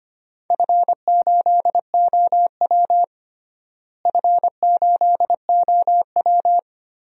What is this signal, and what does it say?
Morse code, heard as a keyed tone, about 25 wpm, 700 Hz. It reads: F8OW F8OW
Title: DXCC-100 Call Signs - 4 Characters - Repeated 3x - 25wpm